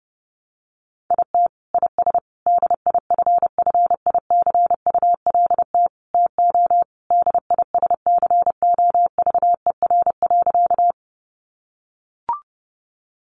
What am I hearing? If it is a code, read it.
Morse code, heard as a keyed tone, about 28 wpm, 700 Hz. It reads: IT IS DIFFICULT TO DISCOVER.
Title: Sentences from Top 1100 Words 30wpm